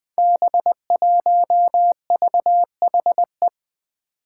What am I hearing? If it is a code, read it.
Morse code, heard as a keyed tone, about 20 wpm, 700 Hz. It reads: B1VHE
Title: Five Challenging Letter-Number Combinations 20wpm